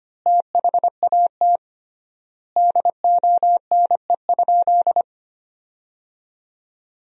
Morse code, heard as a keyed tone, about 25 wpm, 700 Hz. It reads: THAT DONE?
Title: 3x Word-Spacing - Sentences from Top 500 Words - 25wpm